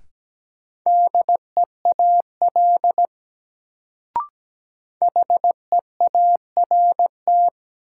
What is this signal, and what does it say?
Morse code, heard as a keyed tone, about 17 wpm, 700 Hz. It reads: DEAL HEART
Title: Sets of 3 Words - New Word Drill - 17wpm